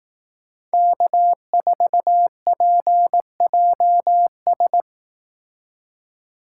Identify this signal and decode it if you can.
Morse code, heard as a keyed tone, about 18 wpm, 700 Hz. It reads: K4PJS
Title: Speed Racing - USA 1x3 Call Signs 15wpm